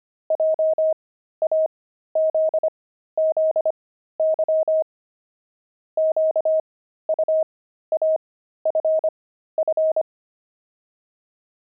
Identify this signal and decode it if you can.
Morse code, heard as a keyed tone, about 25 wpm, spaced as for 13 wpm, 630 Hz. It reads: JAZZY QUAFF
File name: pangram-fs-13-cs-25-saxophones